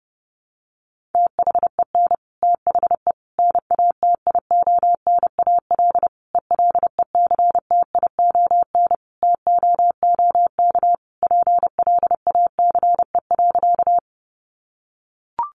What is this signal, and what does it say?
Morse code, heard as a keyed tone, about 30 wpm, 700 Hz. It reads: THEN THE NATIONAL ELECTION TOOK PLACE.
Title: Sentences from Top 1500 Words 30wpm